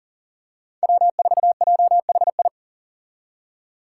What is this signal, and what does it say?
Morse code, heard as a keyed tone, about 38 wpm, 700 Hz. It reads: W4JHI
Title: Five Challenging Letter-Number Combinations 40wpm